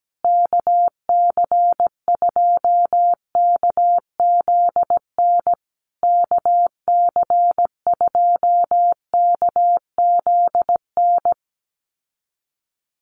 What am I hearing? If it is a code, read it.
Morse code, heard as a keyed tone, about 17 wpm, 700 Hz. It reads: KC2KZN KC2KZN
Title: USA Call Signs Repeated 3x 17wpm